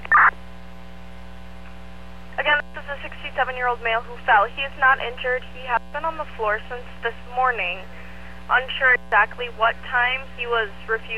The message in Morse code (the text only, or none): none